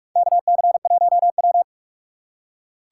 Morse code, heard as a keyed tone, about 45 wpm, 700 Hz. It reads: KC1W
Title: USA Call Signs - 4 Characters - 3x Character Spacing - 45wpm